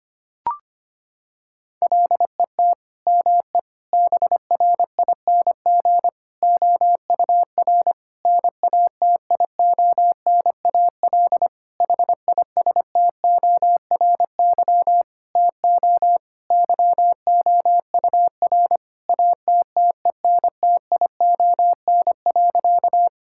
Morse code, heard as a keyed tone, about 25 wpm, 700 Hz. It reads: LET ME BRING OUR NATIONAL HISTORY TO YOUR ATTENTION.
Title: Sentences from Top 1000 Words - Review Part 6 - 25wpm